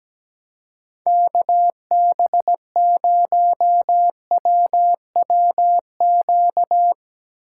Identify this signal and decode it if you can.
Morse code, heard as a keyed tone, about 17 wpm, 700 Hz. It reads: KB0WWQ
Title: Speed Racing - USA Call Signs 17wpm